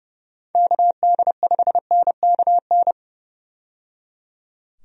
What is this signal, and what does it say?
Morse code, heard as a keyed tone, about 29 wpm, 700 Hz. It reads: KD5NKN
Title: USA Call Signs - 6 Characters - 30wpm